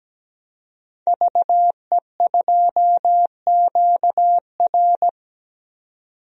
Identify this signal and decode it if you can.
Morse code, heard as a keyed tone, about 17 wpm, 700 Hz. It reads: VE2QR
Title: DXCC-100 Call Signs - 5 Characters - 17wpm